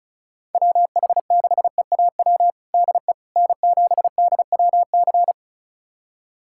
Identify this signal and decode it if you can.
Morse code, heard as a keyed tone, about 36 wpm, 700 Hz. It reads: WH6EAW DE N7DWC
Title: USA Callsign DE Callsign 35wpm